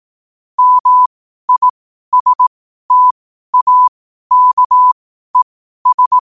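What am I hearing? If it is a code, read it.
Morse code, heard as a keyed tone, about 18 wpm, spaced as for 12 wpm, 1000 Hz. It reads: MISTAKES
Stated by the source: Morse question-of-the-day sent at 12 wpm.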